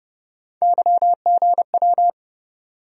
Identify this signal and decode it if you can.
Morse code, heard as a keyed tone, about 30 wpm, 700 Hz. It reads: YGW
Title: USA Call Sign Postfixes - 3 Letters 30wpm